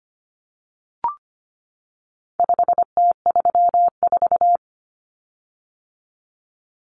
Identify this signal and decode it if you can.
Morse code, heard as a keyed tone, about 25 wpm, 700 Hz. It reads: 5T34
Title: Common Cut Numbers - 4-Digit 25wpm